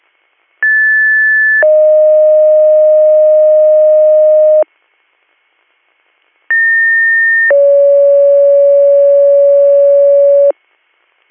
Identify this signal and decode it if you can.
Two-tone page, heard as a one-second tone, then a three-second tone, about 1750 then 620 Hz, about 1800 then 560 Hz. It is heard from 0.6 to 10.5 s.
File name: 200- .0-call 6932